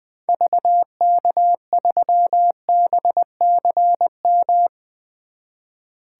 Morse code, heard as a keyed tone, about 20 wpm, 700 Hz. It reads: VK3BCM
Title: 2023 SOTA Activator Call Signs - 20wpm